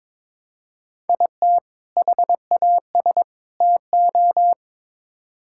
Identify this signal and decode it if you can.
Morse code, heard as a keyed tone, about 22 wpm, 700 Hz. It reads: IT HAS TO
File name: practice / 22wpm / Build A Phrase - 5-Word Phrases - 22wpm